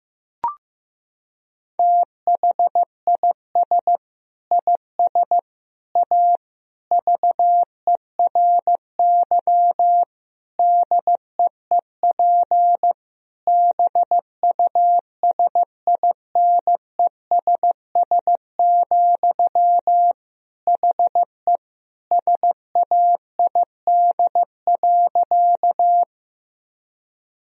Morse code, heard as a keyed tone, about 15 wpm, 700 Hz. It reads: THIS IS A VERY DEEP BUSINESS, HE SAID.
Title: Sentences from Top 1000 Words - Review Part 3 - 15wpm